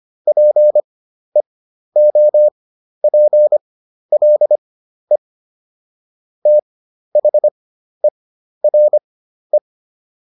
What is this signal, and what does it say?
Morse code, heard as a keyed tone, about 25 wpm, spaced as for 12 wpm, 600 Hz. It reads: PEOPLE THERE